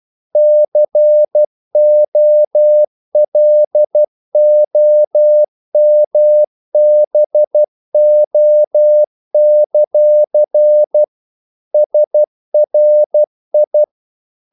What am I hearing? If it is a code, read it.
Morse code, heard as a keyed tone, about 12 wpm, 600 Hz. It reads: COLOMBO; SRI
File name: CW / Practice / NationalCapitals12